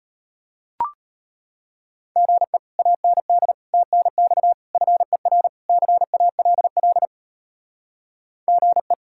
Morse code, heard as a keyed tone, about 33 wpm, 700 Hz. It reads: GE AND TNX FER CALL GE